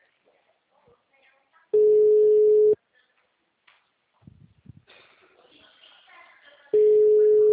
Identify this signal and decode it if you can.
Ringback tone (Europe, 425 Hz), with speech.
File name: out- -2041- - - .24199